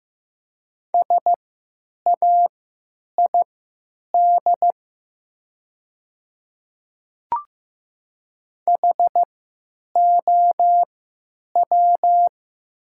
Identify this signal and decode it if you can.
Morse code, heard as a keyed tone, about 15 wpm, spaced as for 9 wpm, 700 Hz. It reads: SAID HOW
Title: 3x Character Spacing - Top 100 Words - 15wpm